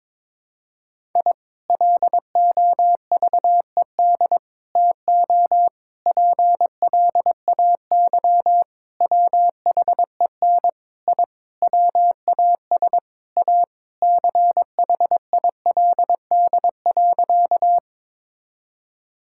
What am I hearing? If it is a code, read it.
Morse code, heard as a keyed tone, about 22 wpm, 700 Hz. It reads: I LOVED TO PLAY WHEN I WAS A CHILD.
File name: practice / 22wpm / Sentences with 4-Word Phrases - 22wpm